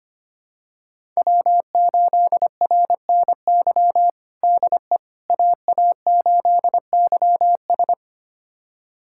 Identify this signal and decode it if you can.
Morse code, heard as a keyed tone, about 25 wpm, 700 Hz. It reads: W8RNY DE AA8YS
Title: USA Callsign DE Callsign 25wpm